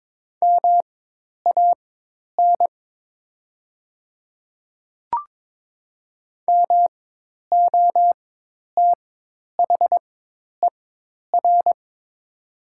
Morse code, heard as a keyed tone, about 22 wpm, spaced as for 10 wpm, 700 Hz. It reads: MAN MOTHER
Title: 4x Character Spacing - Top 200 Words - 22wpm